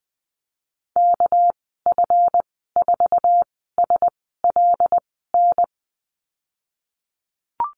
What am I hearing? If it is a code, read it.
Morse code, heard as a keyed tone, about 20 wpm, spaced as for 15 wpm, 700 Hz. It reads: KF4SLN
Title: 200 Lesson 32g - Letter k - Call Signs 20wpm